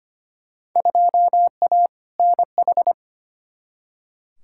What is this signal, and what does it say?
Morse code, heard as a keyed tone, about 25 wpm, 700 Hz. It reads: 2A NH
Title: Field Day Exchanges Encore - 25wpm